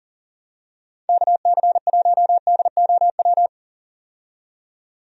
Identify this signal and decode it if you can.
Morse code, heard as a keyed tone, about 38 wpm, 700 Hz. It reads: KC1DOW
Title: USA Call Signs - 6 Characters - 40wpm